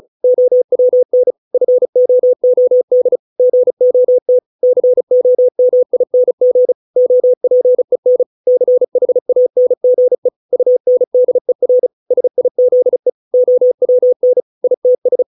Morse code, heard as a keyed tone, about 36 wpm, 500 Hz. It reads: OWN FOOD GOT COMING OPEN CHANGE UNDER SIZE OWN ITS
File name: 632 words / 632 Words 35WPM